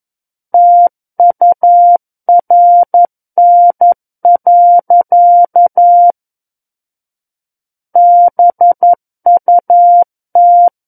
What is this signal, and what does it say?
Morse code, heard as a keyed tone, about 11 wpm, 700 Hz. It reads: TURN. BUT
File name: 11 WPM